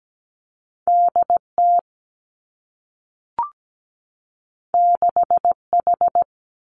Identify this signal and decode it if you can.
Morse code, heard as a keyed tone, about 17 wpm, 700 Hz. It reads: D T 6 H
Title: Two Challenging Letter-Number Combinations 17wpm